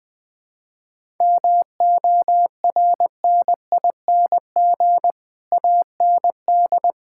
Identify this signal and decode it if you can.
Morse code, heard as a keyed tone, about 20 wpm, 700 Hz. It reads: MORNING AND
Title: Sets of 9 Words - 20wpm